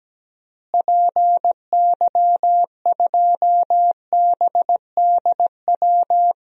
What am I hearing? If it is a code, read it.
Morse code, heard as a keyed tone, about 17 wpm, 700 Hz. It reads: PY2BDW
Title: DXCC-100 Call Signs - 6 Characters - Repeated 3x - 17wpm